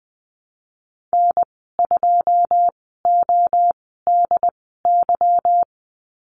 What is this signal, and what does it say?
Morse code, heard as a keyed tone, about 20 wpm, spaced as for 15 wpm, 700 Hz. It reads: N2ODY
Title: 147 Lesson 24g - Letter y - Call Signs 20wpm